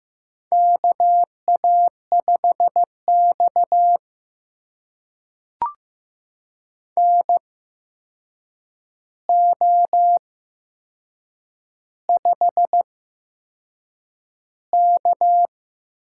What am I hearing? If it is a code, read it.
Morse code, heard as a keyed tone, about 15 wpm, spaced as for 6 wpm, 700 Hz. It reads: KA5X NO5K
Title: USA Call Signs - 4 Characters - 8x Character Spacing - 15wpm